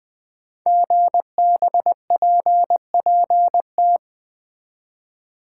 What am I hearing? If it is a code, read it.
Morse code, heard as a keyed tone, about 20 wpm, 700 Hz. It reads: GBPPT
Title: Code Groups - 5 Letters - 20wpm